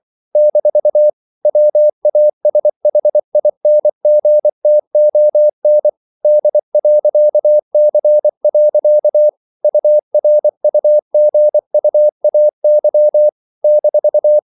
Morse code, heard as a keyed tone, about 24 wpm, 600 Hz. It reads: - WASHINGTON D.C. URUGUAY -
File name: CW / Practice / NationalCapitals24